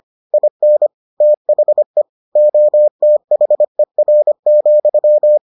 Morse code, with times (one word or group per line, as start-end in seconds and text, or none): 0.3-0.9 IN
1.2-2.0 THE
2.4-5.4 OTHER,